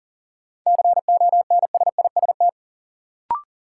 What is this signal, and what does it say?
Morse code, heard as a keyed tone, about 40 wpm, 700 Hz. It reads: CONSIST
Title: Sentences from Top 1400 Words - New Word Drill - 40wpm